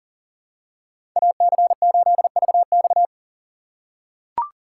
Morse code, heard as a keyed tone, about 37 wpm, 700 Hz. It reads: AC8VX
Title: USA Call Signs - 5 Characters - 4x Character Spacing - 40wpm